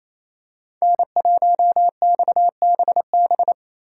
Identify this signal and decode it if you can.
Morse code, heard as a keyed tone, about 28 wpm, 700 Hz. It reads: N1XBB
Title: USA Call Signs - 5 Characters - 2x Character Spacing - 28wpm